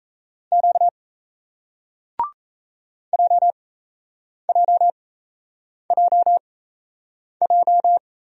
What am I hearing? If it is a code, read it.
Morse code, heard as a keyed tone, about 38 wpm, spaced as for 8 wpm, 700 Hz. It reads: Q JJJJ